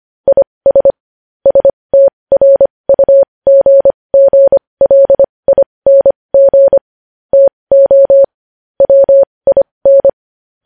Morse code, heard as a keyed tone, about 25 wpm, spaced as for 20 wpm, 550 Hz. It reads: IS STRUGGLING TO WIN
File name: podcast / News headlines-1x-25x20